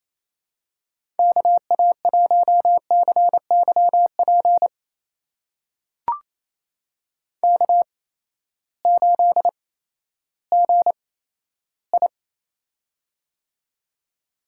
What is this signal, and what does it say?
Morse code, heard as a keyed tone, about 28 wpm, spaced as for 14 wpm, 700 Hz. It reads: KA1CYP K8GI
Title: USA Call Signs - 8x Character Spacing - 28wpm